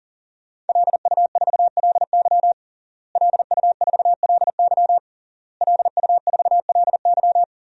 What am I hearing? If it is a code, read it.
Morse code, heard as a keyed tone, about 37 wpm, 700 Hz. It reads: LU4LY LU4LY LU4LY
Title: DXCC-100 Call Signs - 5 Characters - Repeated 3x - 40wpm